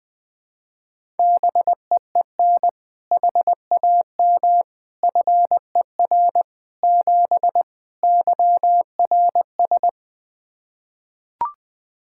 Morse code, heard as a keyed tone, about 20 wpm, 700 Hz. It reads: BEEN HAM FER 7 YRS
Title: K1USN SST QSO Practice 20wpm